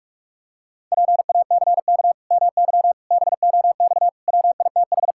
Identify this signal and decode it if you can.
Morse code, heard as a keyed tone, about 46 wpm, 700 Hz. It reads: PACK MY BOX WITH
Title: Pangram Challenge v2 - 45wpm